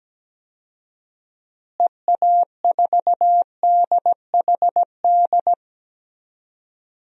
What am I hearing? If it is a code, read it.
Morse code, heard as a keyed tone, about 17 wpm, 700 Hz. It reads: EA4DHD
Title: DXCC-100 Call Signs - 6 Characters - Repeated 2x - 17wpm